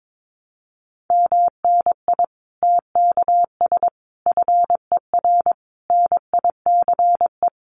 Morse code, heard as a keyed tone, about 22 wpm, 700 Hz. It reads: MNI TKS FER NICE
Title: Common QSO Phrases 22wpm